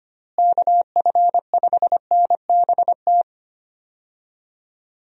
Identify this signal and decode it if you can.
Morse code, heard as a keyed tone, about 25 wpm, 700 Hz. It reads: KF5NBT
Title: USA Call Signs - 6 Characters - Repeated 3x - 25wpm